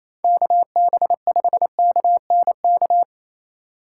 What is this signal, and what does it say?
Morse code, heard as a keyed tone, about 28 wpm, 700 Hz. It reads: KB5KNK
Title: USA Call Signs - 6 Characters - 3x Character Spacing - 28wpm